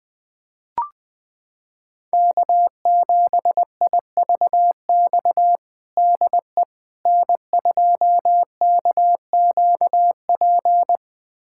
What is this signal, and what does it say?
Morse code, heard as a keyed tone, about 20 wpm, 700 Hz. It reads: K7IVX DE N2KQP
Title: USA Callsign DE Callsign - 5 Characters - 20wpm